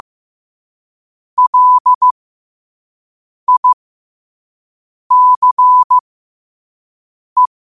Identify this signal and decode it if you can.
Morse code, heard as a keyed tone, about 15 wpm, spaced as for 5 wpm, 1000 Hz. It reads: LICE
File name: qotdfiles / 5wpm